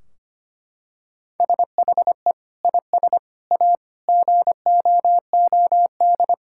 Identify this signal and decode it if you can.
Morse code, heard as a keyed tone, about 25 wpm, 700 Hz. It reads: SHE IS A GOOD
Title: Sentences from Top 1600 Words 25wpm